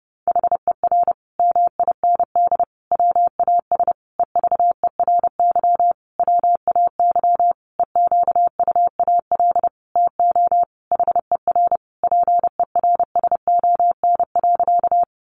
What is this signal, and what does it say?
Morse code, heard as a keyed tone, about 30 wpm, 700 Hz. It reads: HER MIND WAS EVERY WAY EQUAL TO HER PERSON.
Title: Sentences from Top 1400 Words 30wpm